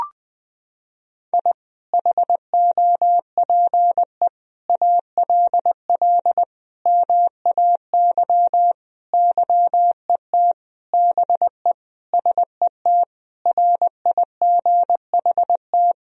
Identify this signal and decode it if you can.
Morse code, heard as a keyed tone, about 20 wpm, 700 Hz. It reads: I HOPE ALL MAY YET BE SET RIGHT
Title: Sentences from Top 1000 Words - Review Part 1 - 20wpm